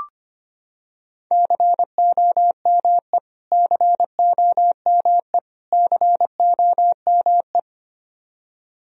Morse code, heard as a keyed tone, about 25 wpm, 700 Hz. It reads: COME COME COME
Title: Top 100 Words Repeated 25wpm